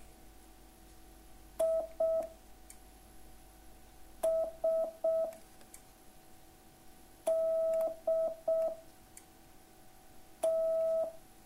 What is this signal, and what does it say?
Morse code, heard as a keyed tone, about 6 wpm, 660 Hz. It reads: I S D T